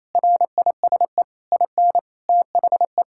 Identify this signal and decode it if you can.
Morse code, heard as a keyed tone, about 28 wpm, 700 Hz. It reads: RISE IN THE
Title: Sentences from Top 1000 Words - Review Part 6 - 28wpm